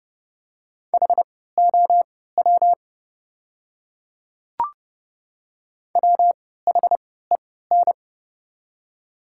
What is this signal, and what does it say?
Morse code, heard as a keyed tone, about 30 wpm, spaced as for 17 wpm, 700 Hz. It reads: HOW WHEN
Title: 3x Character Spacing - Top 100 Words - 30wpm